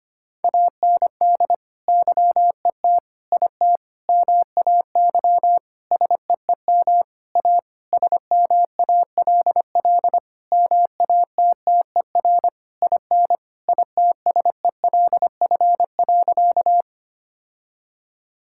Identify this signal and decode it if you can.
Morse code, heard as a keyed tone, about 25 wpm, 700 Hz. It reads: AND YET IT MAY SEEM A SMALL MATTER IN ITSELF.
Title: Sentences from Top 1000 Words - Review Part 5 - 25wpm